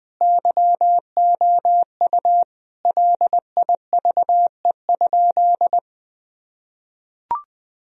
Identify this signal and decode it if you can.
Morse code, heard as a keyed tone, about 21 wpm, 700 Hz. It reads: YOU LIVE?
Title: Sentences from Top 1000 Words - Review Part 4 - 20wpm